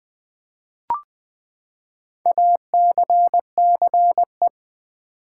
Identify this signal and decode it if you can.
Morse code, heard as a keyed tone, about 20 wpm, 700 Hz. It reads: ACCE
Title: Build A Word - 4 Letters At A Time - 10-Letter Words - 20wpm